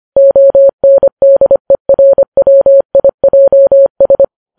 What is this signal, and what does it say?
Morse code, heard as a keyed tone, about 25 wpm, 550 Hz. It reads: ONDERWIJS